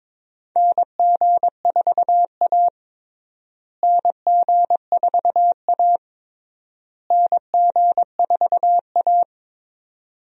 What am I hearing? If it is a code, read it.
Morse code, heard as a keyed tone, about 22 wpm, 700 Hz. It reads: NG4A NG4A NG4A
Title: USA Call Signs - 4 Characters - Repeated 3x - 22wpm